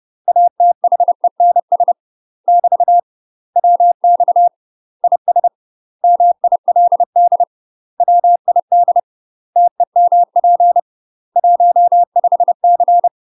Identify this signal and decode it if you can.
Morse code, heard as a keyed tone, about 30 wpm, 700 Hz. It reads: ATHENS <BT> WX IS MILD WID TEMP 15C
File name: UK MASTER QSOs TRAINING / 30WPM / UK MASTER QSOs 0047